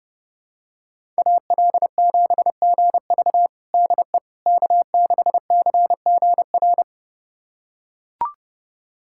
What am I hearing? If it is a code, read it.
Morse code, heard as a keyed tone, about 30 wpm, 700 Hz. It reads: AL7GV DE K6CGR
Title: USA Callsign DE Callsign - 5 Characters - 30wpm